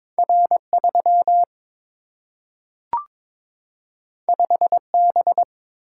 Morse code, heard as a keyed tone, about 22 wpm, 700 Hz. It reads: R 3 5 B
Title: Single-Letter Single-Number 22wpm